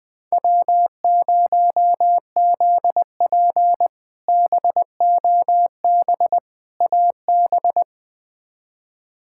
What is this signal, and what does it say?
Morse code, heard as a keyed tone, about 20 wpm, 700 Hz. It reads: W0ZP BOB AB
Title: K1USN SST QSO Practice - 20wpm